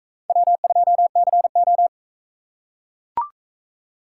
Morse code, heard as a keyed tone, about 40 wpm, 700 Hz. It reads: W2CO